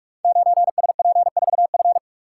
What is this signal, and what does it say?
Morse code, heard as a keyed tone, about 42 wpm, 700 Hz. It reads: 9SP4F
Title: Code Groups - 5 Letter:Numbers - 45wpm